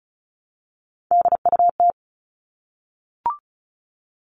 Morse code, heard as a keyed tone, about 34 wpm, 700 Hz. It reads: DUT
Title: All Cut Numbers - 3-Digit 35wpm